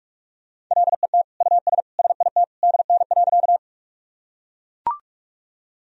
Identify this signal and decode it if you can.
Morse code, heard as a keyed tone, about 41 wpm, 700 Hz. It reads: LET US SIT DN.
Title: Sentences from Top 500 Words and Abbreviations 45wpm